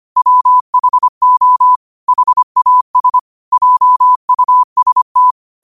Morse code, heard as a keyed tone, about 25 wpm, 1000 Hz. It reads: WHO HAS JUST
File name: qotdfiles / 25wpm